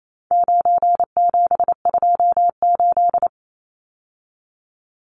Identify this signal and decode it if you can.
Morse code, heard as a keyed tone, about 28 wpm, 700 Hz. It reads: 9728
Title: Numbers - 4-Digit 28wpm